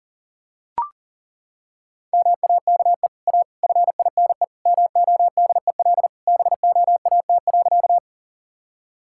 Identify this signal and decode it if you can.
Morse code, heard as a keyed tone, about 40 wpm, 700 Hz. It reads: MAKE A FINE MODEL BOAT.
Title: Sentences from Top 1000 Words - Review Part 5 - 40wpm